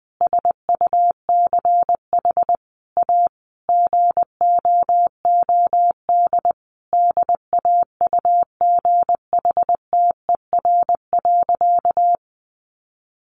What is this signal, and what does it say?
Morse code, heard as a keyed tone, about 20 wpm, 700 Hz. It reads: SUCH A GOOD DAUGHTER.
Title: Sentences from Top 1500 Words 20wpm